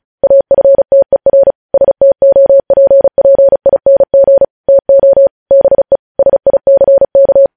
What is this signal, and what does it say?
Morse code, heard as a keyed tone, about 36 wpm, 550 Hz. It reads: AFTER STOPPING TO BE SICK
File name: podcast / News headlines-1x-35x35